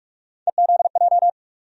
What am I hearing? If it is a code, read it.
Morse code, heard as a keyed tone, about 42 wpm, 700 Hz. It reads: EZJ
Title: USA Call Sign Postfixes - 3 Letters 45wpm